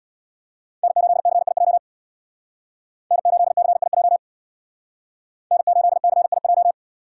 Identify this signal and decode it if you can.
Morse code, heard as a keyed tone, about 56 wpm, 700 Hz. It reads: N8CIJ N8CIJ N8CIJ
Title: Speed Racing - USA 1x3 Call Signs 50wpm